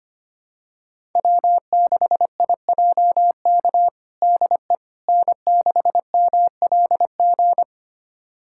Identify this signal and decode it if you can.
Morse code, heard as a keyed tone, about 25 wpm, 700 Hz. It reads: W6IJK DE N6MLG
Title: USA Callsign DE Callsign - 5 Characters - 25wpm